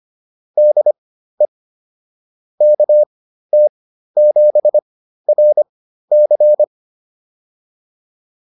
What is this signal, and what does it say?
Morse code, heard as a keyed tone, about 25 wpm, spaced as for 13 wpm, 600 Hz. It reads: DE KT7RC
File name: POTA207 13